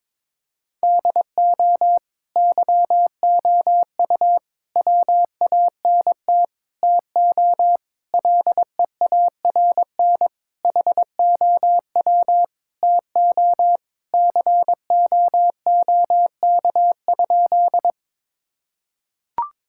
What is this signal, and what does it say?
Morse code, heard as a keyed tone, about 22 wpm, 700 Hz. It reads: DO YOU WANT TO LEARN HOW TO COOK?
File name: Sentences with 7-Word Phrases - 22wpm